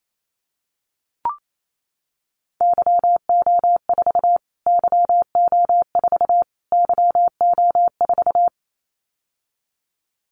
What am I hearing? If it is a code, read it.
Morse code, heard as a keyed tone, about 28 wpm, 700 Hz. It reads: YO4 YO4 YO4
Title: Callsign Prefixes for Top 25 DXCC - Repeated 3x - 28wpm